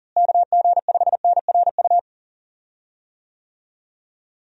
Morse code, heard as a keyed tone, about 40 wpm, 700 Hz. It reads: KG5NRU
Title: USA Call Signs - 6 Characters - Repeated 3x - 40wpm